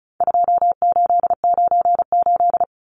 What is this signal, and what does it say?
Morse code, heard as a keyed tone, about 34 wpm, 700 Hz. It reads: 2898
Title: Numbers - 4-Digit 35wpm